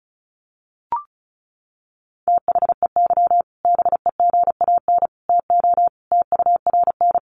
Morse code, heard as a keyed tone, about 34 wpm, 700 Hz. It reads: THEY BEGAN TO TURN
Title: Sentences from Top 1100 Words 35wpm